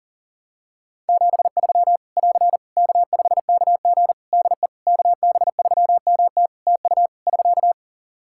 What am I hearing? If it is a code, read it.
Morse code, heard as a keyed tone, about 40 wpm, 700 Hz. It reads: 73 <AR> K5KG DE KB3MT TU <SK>
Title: K1USN SST QSO Practice 40wpm